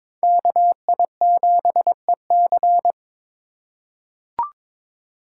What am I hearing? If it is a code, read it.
Morse code, heard as a keyed tone, about 22 wpm, 700 Hz. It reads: KI7EC
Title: USA Call Signs - 6x Character Spacing - 22wpm